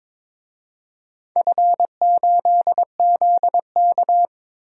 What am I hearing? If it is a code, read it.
Morse code, heard as a keyed tone, about 22 wpm, 700 Hz. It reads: F8ZK
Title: DXCC-100 Call Signs - 4 Characters - 22wpm